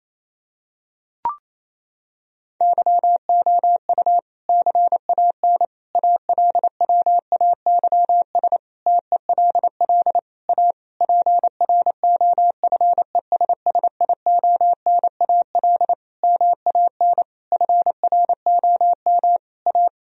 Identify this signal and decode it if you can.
Morse code, heard as a keyed tone, about 28 wpm, 700 Hz. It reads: YOU CAN ALWAYS TELL A PROFESSIONAL MAN FROM A
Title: Sentences from Top 1000 Words - Review Part 4 - 28wpm